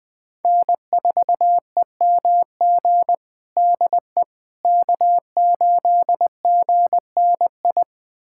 Morse code, heard as a keyed tone, about 21 wpm, 700 Hz. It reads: N4EMG DE K8GNI
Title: USA Callsign DE Callsign - 5 Characters - 20wpm